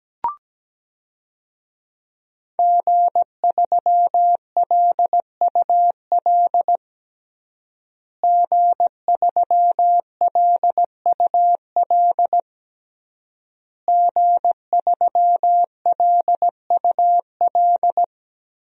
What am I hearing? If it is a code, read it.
Morse code, heard as a keyed tone, about 17 wpm, 700 Hz. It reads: G3LUL G3LUL G3LUL
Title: DXCC-100 Call Signs - 5 Characters - Repeated 3x - 17wpm